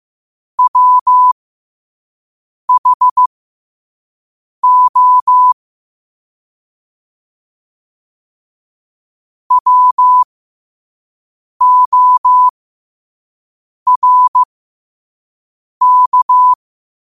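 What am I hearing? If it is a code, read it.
Morse code, heard as a keyed tone, about 15 wpm, spaced as for 5 wpm, 1000 Hz. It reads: WHO WORK
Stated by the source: Morse code sent at 5 wpm.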